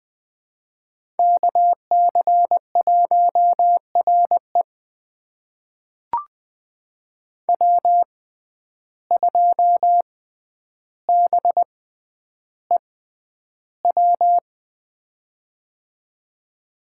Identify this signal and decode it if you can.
Morse code, heard as a keyed tone, about 20 wpm, spaced as for 10 wpm, 700 Hz. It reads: KC1RE W2BEW
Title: USA Call Signs - 5 Characters - 6x Character Spacing - 20wpm